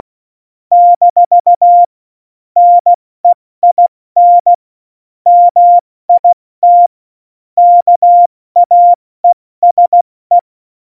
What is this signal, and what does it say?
Morse code, heard as a keyed tone, about 16 wpm, spaced as for 14 wpm, 700 Hz. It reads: - NEIN MIT KAESE
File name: Morsekurs / Witze / Witz 24. Oktober 17, Tempo 70